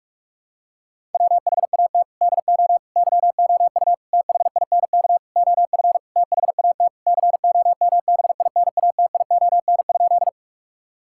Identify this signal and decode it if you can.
Morse code, heard as a keyed tone, about 41 wpm, 690 Hz. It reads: WHAT DO YOU THINK OF THAT COMBINATION?
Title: Sentences from Top 1000 Words - Review Part 6 - 45wpm